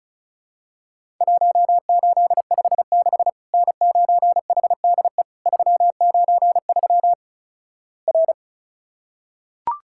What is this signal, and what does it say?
Morse code, heard as a keyed tone, about 36 wpm, 700 Hz. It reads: 1856 N9HDE 393 R
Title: Worked All Europe DX Contest - QTC Exchange - 35wpm